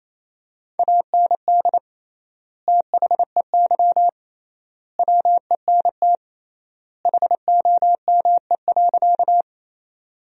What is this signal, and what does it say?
Morse code, heard as a keyed tone, about 28 wpm, 700 Hz. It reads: AND THEY WENT HOME.
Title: 3x Word-Spacing - Sentences from Top 200 Words - 28wpm